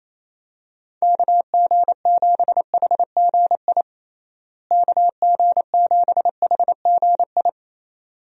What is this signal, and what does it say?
Morse code, heard as a keyed tone, about 28 wpm, 700 Hz. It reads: KG7HGI KG7HGI
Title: USA Call Signs - 6 Characters - Repeated 2x - 28wpm